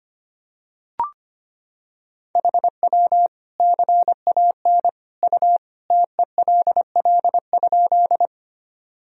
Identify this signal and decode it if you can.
Morse code, heard as a keyed tone, about 25 wpm, 700 Hz. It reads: HW CAN U TELL?
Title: Sentences from Top 200 Words and Abbreviations 25wpm